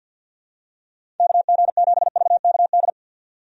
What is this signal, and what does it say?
Morse code, heard as a keyed tone, about 52 wpm, 700 Hz. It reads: KG7VKD
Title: USA Call Signs - 6 Characters - 50wpm